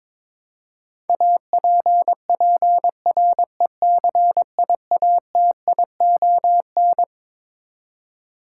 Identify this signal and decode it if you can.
Morse code, heard as a keyed tone, about 22 wpm, 700 Hz. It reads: APPRECIATION
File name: Twelve Letter Words - 22wpm